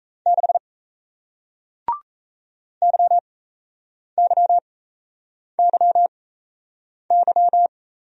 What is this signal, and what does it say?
Morse code, heard as a keyed tone, about 38 wpm, spaced as for 8 wpm, 700 Hz. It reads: 6 YYYY